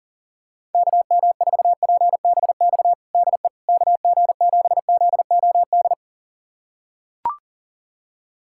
Morse code, heard as a keyed tone, about 40 wpm, 690 Hz. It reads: KM4PBX DE KG7ZOD
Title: USA Callsign DE Callsign - 6 Characters - 40wpm